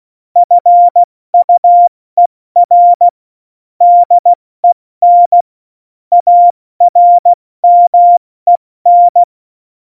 Morse code, heard as a keyed tone, about 16 wpm, 700 Hz. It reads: FUER DEN ARMEN
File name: Morsekurs / Witze / Witz 26. September 17, Tempo 70